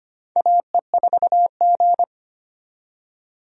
Morse code, heard as a keyed tone, about 25 wpm, 700 Hz. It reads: AE4G
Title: USA Call Signs - 4 Characters - 25wpm